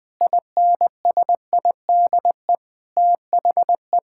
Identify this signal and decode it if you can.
Morse code, heard as a keyed tone, about 20 wpm, 700 Hz. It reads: INSIDE THE
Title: Sentences from Top 1000 Words - Review Part 6 - 20wpm